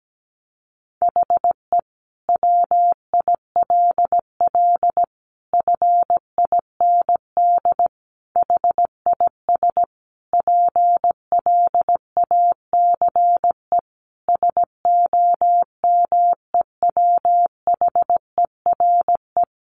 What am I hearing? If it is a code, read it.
Morse code, heard as a keyed tone, about 17 wpm, 700 Hz. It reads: HE WILL FIND HIS PLACE SOMEWHERE
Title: Sentences from Top 1400 Words 17wpm